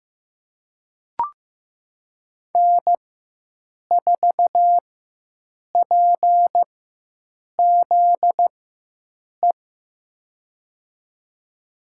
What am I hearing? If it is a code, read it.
Morse code, heard as a keyed tone, about 15 wpm, spaced as for 7 wpm, 700 Hz. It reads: N4PZE
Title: USA Call Signs - 5 Characters - 4x Character Spacing - 15wpm